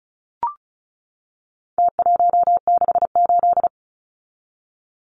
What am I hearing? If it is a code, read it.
Morse code, heard as a keyed tone, about 34 wpm, 700 Hz. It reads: T168
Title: Numbers - Contest Style - 4-Digit 35wpm